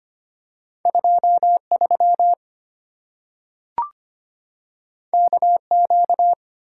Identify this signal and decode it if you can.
Morse code, heard as a keyed tone, about 25 wpm, 690 Hz. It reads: 23 KQ
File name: Two Letter-Number Combinations - 25wpm